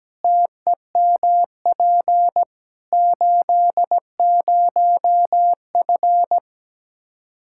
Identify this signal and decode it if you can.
Morse code, heard as a keyed tone, about 17 wpm, 700 Hz. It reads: TEMP 80F
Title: K1USN SST QSO Practice 17wpm